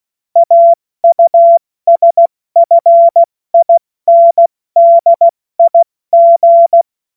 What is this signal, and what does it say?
Morse code, heard as a keyed tone, about 16 wpm, 680 Hz. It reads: AUSFINDIG
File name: Morsekurs/Witze